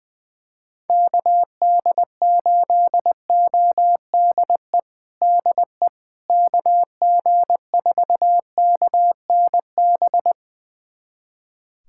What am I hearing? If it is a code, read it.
Morse code, heard as a keyed tone, about 20 wpm, 700 Hz. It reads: KD8ODE DE KG4KNB
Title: USA Callsign DE Callsign - 6 Characters - 20wpm